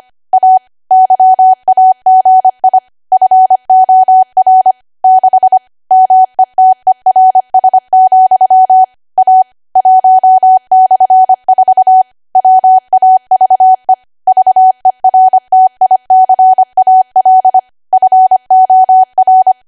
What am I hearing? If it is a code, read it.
Morse code, heard as a keyed tone, about 25 wpm, 750 Hz. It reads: A YAGI FOR 6 METERS, A 1/4 WAVE VERTICAL FOR
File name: Archive / 25 WPM / 25WPM